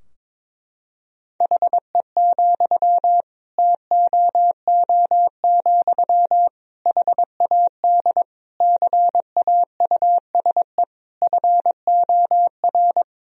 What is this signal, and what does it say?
Morse code, heard as a keyed tone, about 23 wpm, 700 Hz. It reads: HE, TOO, HAD CAUSE FOR
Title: Sentences from Top 1000 Words - Review Part 3 - 22wpm